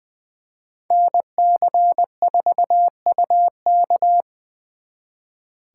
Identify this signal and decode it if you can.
Morse code, heard as a keyed tone, about 24 wpm, 700 Hz. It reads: NC4UK
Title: Speed Racing - USA 2x2 Call Signs 20wpm